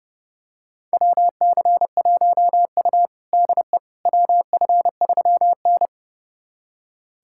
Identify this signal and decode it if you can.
Morse code, heard as a keyed tone, about 32 wpm, 700 Hz. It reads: WC1U DE WF3N
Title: USA Callsign DE Callsign - 4 Characters - 30wpm